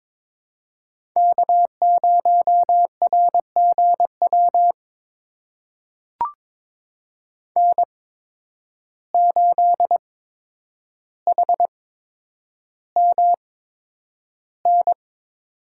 Morse code, heard as a keyed tone, about 22 wpm, spaced as for 9 wpm, 700 Hz. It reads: K0RGW N8HMN
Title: USA Call Signs - 5 Characters - 8x Character Spacing - 22wpm